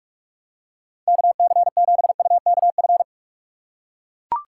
Morse code, heard as a keyed tone, about 42 wpm, 700 Hz. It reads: KC7UKF
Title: USA Call Signs - 6 Characters - 3x Character Spacing - 45wpm